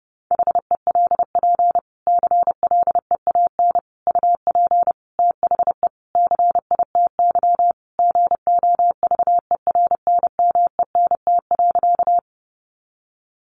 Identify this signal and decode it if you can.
Morse code, heard as a keyed tone, about 30 wpm, 700 Hz. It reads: HELP CLEAN UP THE CITY GOVERNMENT.
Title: Sentences from Top 1100 Words 30wpm